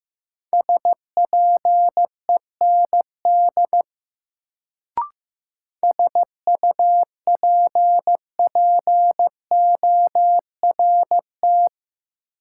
Morse code, heard as a keyed tone, about 15 wpm, 700 Hz. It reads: SPEND SUPPORT
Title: Sentences from Top 800 Words - New Word Drill - 15wpm